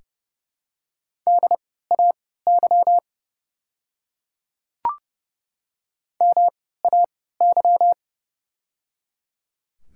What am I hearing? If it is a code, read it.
Morse code, heard as a keyed tone, about 30 wpm, spaced as for 17 wpm, 700 Hz. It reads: DAY MAY
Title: 3x Character Spacing - Top 100 Words - 30wpm